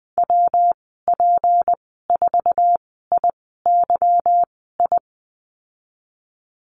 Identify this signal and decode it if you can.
Morse code, heard as a keyed tone, about 19 wpm, spaced as for 14 wpm, 700 Hz. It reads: WP4IYI
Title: 179 Lesson 29g - Slash - Call Signs 20wpm